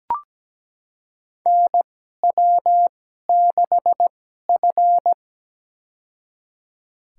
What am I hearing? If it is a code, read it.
Morse code, heard as a keyed tone, about 17 wpm, spaced as for 12 wpm, 700 Hz. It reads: NW6F
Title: USA Call Signs - 4 Characters - 2x Character Spacing - 17wpm